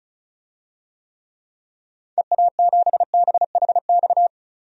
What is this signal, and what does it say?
Morse code, heard as a keyed tone, about 34 wpm, 700 Hz. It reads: EA7BHX
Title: DXCC-100 Call Signs - 6 Characters - 35wpm